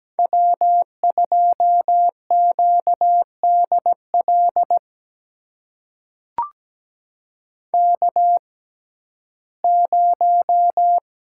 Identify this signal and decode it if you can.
Morse code, heard as a keyed tone, about 17 wpm, spaced as for 12 wpm, 700 Hz. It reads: W2QDL K0
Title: USA Call Signs - 5 Characters - 6x Character Spacing - 17wpm